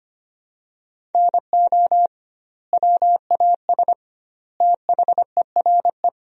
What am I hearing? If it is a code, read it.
Morse code, heard as a keyed tone, about 25 wpm, 700 Hz. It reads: NO WAS THERE
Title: 2x Word-Spacing - Sentences from Top 100 Words - 25wpm